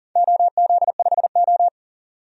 Morse code, heard as a keyed tone, about 41 wpm, 700 Hz. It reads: OZ5O
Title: DXCC-100 Call Signs - 4 Characters - 40wpm